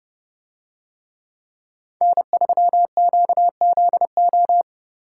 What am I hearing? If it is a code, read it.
Morse code, heard as a keyed tone, about 30 wpm, 700 Hz. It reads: N3QZO